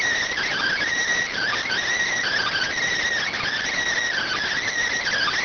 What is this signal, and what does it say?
Emergency Alert System (SAME) tones (header bursts), heard as harsh buzzing bursts. It is heard from 0.3 to 4.7 s.